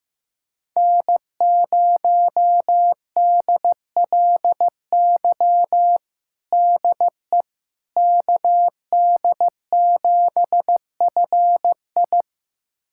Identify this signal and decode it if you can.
Morse code, heard as a keyed tone, about 15 wpm, 700 Hz. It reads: N0DLY DE KD7FI
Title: USA Callsign DE Callsign - 5 Characters - 15wpm